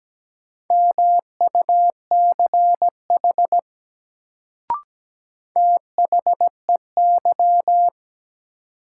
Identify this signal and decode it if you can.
Morse code, heard as a keyed tone, about 17 wpm, 700 Hz. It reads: MUCH THEY
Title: Sets of 2 Words - New Word Drill - 17wpm